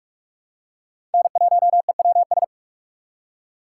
Morse code, heard as a keyed tone, about 44 wpm, 700 Hz. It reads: N1EWS
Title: USA Call Signs - 5 Characters - 6x Character Spacing - 45wpm